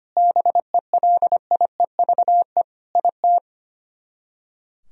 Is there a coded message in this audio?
Yes: Morse code from 0.2 to 3.4 s, heard as a keyed tone, about 32 wpm, 700 Hz. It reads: BELIEVE IT